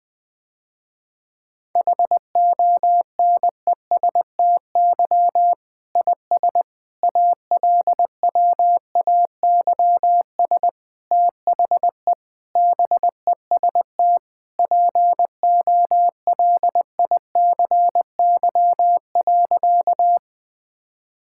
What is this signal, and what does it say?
Morse code, heard as a keyed tone, about 20 wpm, 700 Hz. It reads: HONESTY IS ALWAYS THE BEST POLICY.
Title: Difficult Sentences with Hints - 20wpm